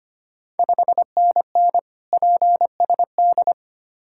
Morse code, heard as a keyed tone, about 25 wpm, 700 Hz. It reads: 5NN PSD
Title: Ontario QSO Party Exchange 25wpm